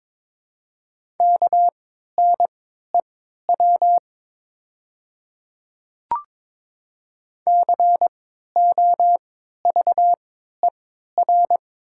Morse code, heard as a keyed tone, about 22 wpm, spaced as for 13 wpm, 700 Hz. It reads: KNEW COVER
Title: 3x Character Spacing - Top 500 Words - 22wpm